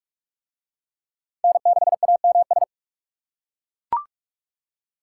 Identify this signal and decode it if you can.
Morse code, heard as a keyed tone, about 40 wpm, 700 Hz. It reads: N6AMS